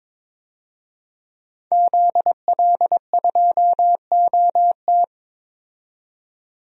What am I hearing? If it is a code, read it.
Morse code, heard as a keyed tone, about 21 wpm, 700 Hz. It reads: ZL2OT
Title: DXCC-100 Call Signs - 5 Characters - 22wpm